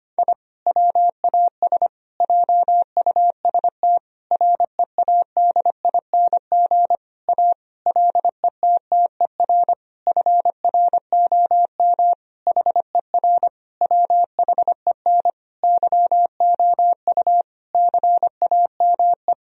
Morse code, heard as a keyed tone, about 25 wpm, 700 Hz. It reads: I WAS JUST READING A LETTER FROM HER WHEN YOU CAME
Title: Sentences from Top 1000 Words - Review Part 3 - 25wpm